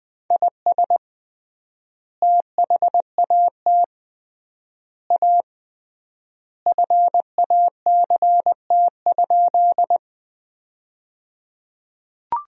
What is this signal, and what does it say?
Morse code, heard as a keyed tone, about 20 wpm, 700 Hz. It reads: IS THAT A FACT?
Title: 3x Word-Spacing - Sentences from Top 500 Words - 20wpm